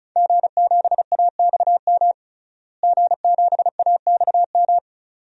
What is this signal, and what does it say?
Morse code, heard as a keyed tone, about 34 wpm, 700 Hz. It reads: G7AXM G7AXM
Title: DXCC-100 Call Signs - 5 Characters - Repeated 2x - 35wpm